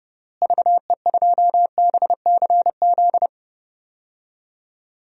Morse code, heard as a keyed tone, about 30 wpm, 700 Hz. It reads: VE2BCZ
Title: DXCC-100 Call Signs - 6 Characters - Repeated 2x - 30wpm